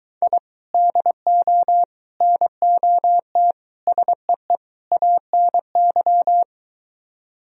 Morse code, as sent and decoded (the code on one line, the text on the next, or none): .. / -.. --- / -. --- - / ... . / .- -. -.--
I DO NOT SEE ANY